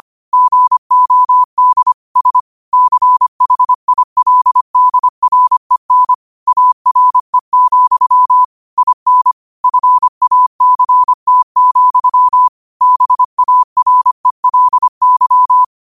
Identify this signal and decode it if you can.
Morse code, heard as a keyed tone, about 25 wpm, 1000 Hz. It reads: GOD S CHILDREN ARE, IN FACT, BARELY
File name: qotdfiles / 25wpm